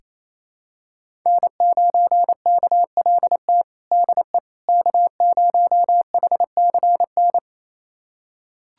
Morse code, heard as a keyed tone, about 28 wpm, 700 Hz. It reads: N9KLT DE K0HCN
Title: USA Callsign DE Callsign - 5 Characters - 28wpm